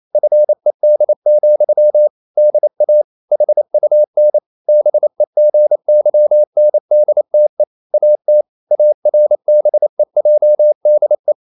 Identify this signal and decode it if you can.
Morse code, heard as a keyed tone, about 28 wpm, 600 Hz. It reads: FED, DA HUN BEGYNDTE AT ARBEJDE
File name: Ferie QRQ 28wpm